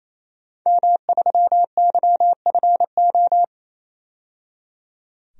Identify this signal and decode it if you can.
Morse code, heard as a keyed tone, about 28 wpm, 700 Hz. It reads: M3YFO
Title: DXCC-100 Call Signs - 5 Characters - 28wpm